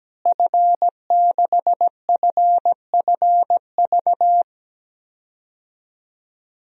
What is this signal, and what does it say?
Morse code, heard as a keyed tone, about 17 wpm, 700 Hz. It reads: F6FFV
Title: DXCC-100 Call Signs - 5 Characters - Repeated 2x - 17wpm